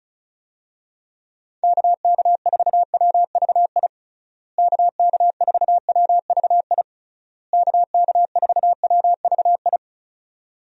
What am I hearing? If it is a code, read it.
Morse code, heard as a keyed tone, about 34 wpm, 700 Hz. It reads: KK4WVI KK4WVI KK4WVI
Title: USA Call Signs - 6 Characters - Repeated 3x - 35wpm